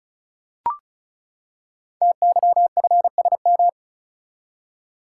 Morse code, heard as a keyed tone, about 36 wpm, 700 Hz. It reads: TYFSM
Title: Code Groups - 5 Letters - 35wpm